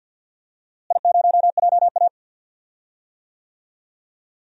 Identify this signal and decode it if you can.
Morse code, heard as a keyed tone, about 48 wpm, 700 Hz. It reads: I0JA
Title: DXCC-100 Call Signs - 4 Characters - Repeated 2x - 50wpm